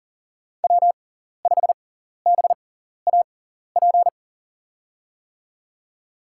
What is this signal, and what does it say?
Morse code, heard as a keyed tone, about 40 wpm, spaced as for 14 wpm, 700 Hz. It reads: W5BAP
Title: USA Call Signs - 6x Character Spacing - 40wpm